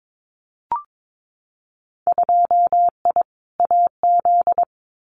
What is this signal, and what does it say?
Morse code, heard as a keyed tone, about 22 wpm, 700 Hz. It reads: 2I AZ
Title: Winter Field Day Exchanges Encore 22wpm